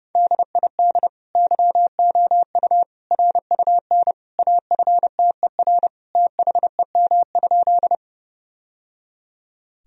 Morse code, heard as a keyed tone, about 30 wpm, 700 Hz. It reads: DID YOU RUN AFTER THEM?
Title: Sentences from Top 1000 Words - Review Part 1 - 30wpm